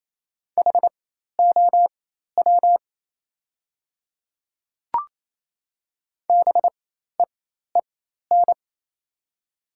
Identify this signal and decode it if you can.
Morse code, heard as a keyed tone, about 28 wpm, spaced as for 13 wpm, 700 Hz. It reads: HOW BEEN
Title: 4x Character Spacing - Top 100 Words - 28wpm